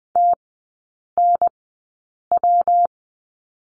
Morse code, heard as a keyed tone, about 20 wpm, spaced as for 8 wpm, 700 Hz. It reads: TNW
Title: 121 Lesson 20e - Number 3 - Three Characters 20wpm